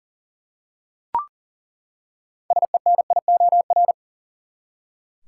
Morse code, heard as a keyed tone, about 37 wpm, 700 Hz. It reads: SENIOR
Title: Six Letter Words - 40wpm